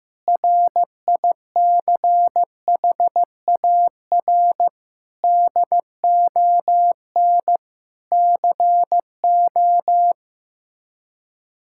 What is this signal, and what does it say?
Morse code, heard as a keyed tone, about 15 wpm, 700 Hz. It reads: RICHAR DON CO
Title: K1USN SST QSO Practice - 15wpm